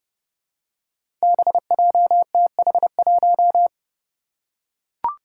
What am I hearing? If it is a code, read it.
Morse code, heard as a keyed tone, about 30 wpm, 700 Hz. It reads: BJTH1
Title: Five Challenging Letter-Number Combinations 30wpm